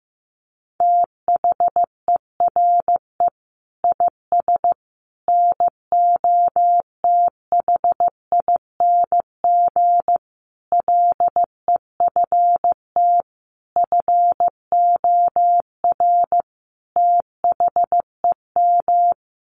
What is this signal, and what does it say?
Morse code, heard as a keyed tone, about 15 wpm, 700 Hz. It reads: THERE IS NOTHING LEFT FOR THEM
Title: Sentences from Top 1100 Words 15wpm